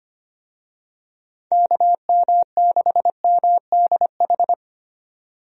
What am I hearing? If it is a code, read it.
Morse code, heard as a keyed tone, about 25 wpm, 700 Hz. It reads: KM6MDH
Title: USA Call Signs - 6 Characters - 25wpm